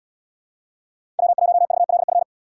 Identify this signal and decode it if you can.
Morse code, heard as a keyed tone, about 78 wpm, 700 Hz. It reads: W1VLF